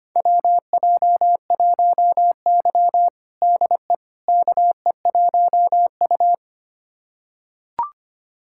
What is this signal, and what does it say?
Morse code, heard as a keyed tone, about 25 wpm, 700 Hz. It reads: WJ1Y DE KE1U
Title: USA Callsign DE Callsign - 4 Characters - 25wpm